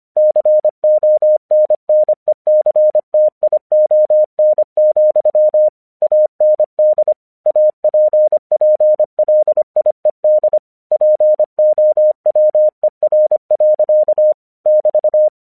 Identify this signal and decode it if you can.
Morse code, heard as a keyed tone, about 25 wpm, 610 Hz. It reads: CONNECTION, AND APPLIED POWER. <BT>
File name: 1 English 25wpm / AQC27